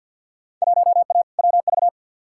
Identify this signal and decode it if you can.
Morse code, heard as a keyed tone, about 48 wpm, 700 Hz. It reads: 1A WV
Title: Field Day Exchanges Encore - 50wpm